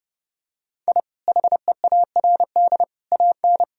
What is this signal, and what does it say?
Morse code, heard as a keyed tone, about 30 wpm, 700 Hz. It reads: I HEARD AN
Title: Sentences from Top 1000 Words - Review Part 4 - 30wpm